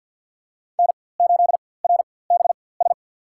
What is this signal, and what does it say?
Morse code, heard as a keyed tone, about 47 wpm, spaced as for 23 wpm, 700 Hz. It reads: N8RBS
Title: USA Call Signs - 4x Character Spacing - 50wpm